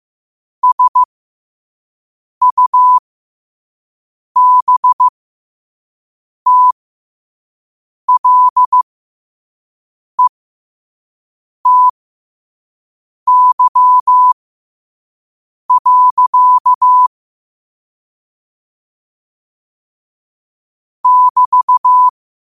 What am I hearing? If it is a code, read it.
Morse code, heard as a keyed tone, about 15 wpm, spaced as for 5 wpm, 1000 Hz. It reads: SUBTLETY. <BT>
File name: qotdfiles / 5wpm